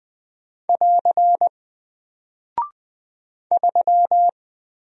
Morse code, heard as a keyed tone, about 20 wpm, 700 Hz. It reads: <AR> 3
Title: Single Character (Letter:Number:Punctuation:Prosign) Alt 20wpm